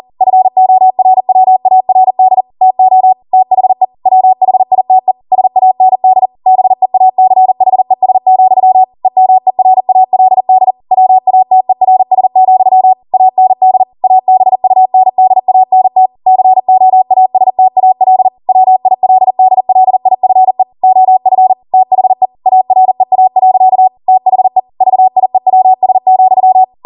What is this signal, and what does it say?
Morse code, heard as a keyed tone, about 40 wpm, 750 Hz. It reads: FORWARD TO THE WHITE SAND BEACHES, EMERALD WATERS, AND ABUNDANT COASTAL WILDLIFE OF THE AREA. THE VIEWS,